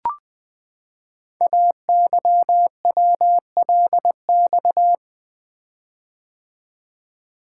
Morse code, heard as a keyed tone, about 20 wpm, 700 Hz. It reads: AYWLX
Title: Code Groups - 5 Letters - 20wpm